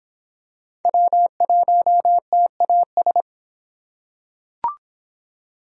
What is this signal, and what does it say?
Morse code, heard as a keyed tone, about 25 wpm, 700 Hz. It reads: W1TAS